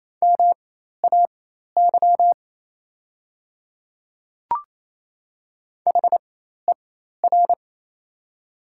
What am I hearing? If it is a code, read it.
Morse code, heard as a keyed tone, about 28 wpm, spaced as for 13 wpm, 700 Hz. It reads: MAY HER
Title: 4x Character Spacing - Top 200 Words - 28wpm